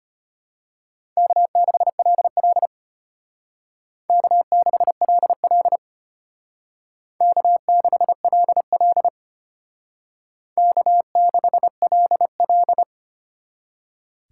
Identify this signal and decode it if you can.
Morse code, heard as a keyed tone, about 31 wpm, 700 Hz. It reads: K6LL K6LL K6LL K6LL
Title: Speed Racing - USA Call Signs 25wpm